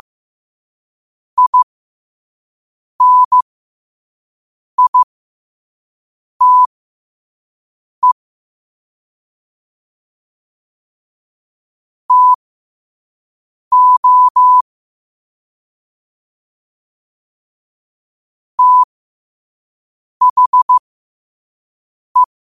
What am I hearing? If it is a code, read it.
Morse code, heard as a keyed tone, about 15 wpm, spaced as for 5 wpm, 1000 Hz. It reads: INITE TO THE
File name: qotdfiles / 5wpm